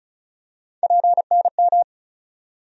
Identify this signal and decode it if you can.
Morse code, heard as a keyed tone, about 36 wpm, 700 Hz. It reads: PNM
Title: USA Call Sign Postfixes - 3 Letters 35wpm